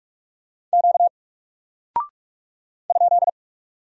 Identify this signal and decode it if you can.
Morse code, heard as a keyed tone, about 46 wpm, 700 Hz. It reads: Q ?
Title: Single Character (Letter:Number:Punctuation:Prosign) Alt 45wpm